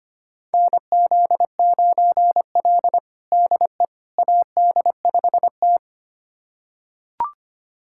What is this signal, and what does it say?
Morse code, heard as a keyed tone, about 25 wpm, 700 Hz. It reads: NZ9L DE AD5T
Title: USA Callsign DE Callsign - 4 Characters - 25wpm